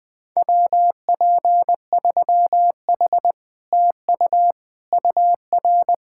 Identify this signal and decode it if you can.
Morse code, heard as a keyed tone, about 20 wpm, 700 Hz. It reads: WP3H TU UR